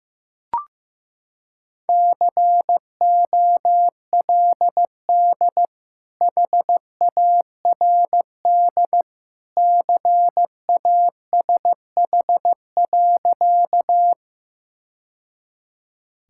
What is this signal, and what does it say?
Morse code, heard as a keyed tone, about 15 wpm, 700 Hz. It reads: COLD HARD CASH.
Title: Sentences from Top 1000 Words - Review Part 5 - 15wpm